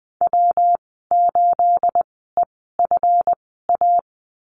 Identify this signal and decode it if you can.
Morse code, heard as a keyed tone, about 20 wpm, spaced as for 14 wpm, 700 Hz. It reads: W8EFA
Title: 231 Lesson 37g - Letter x - Call Signs 20wpm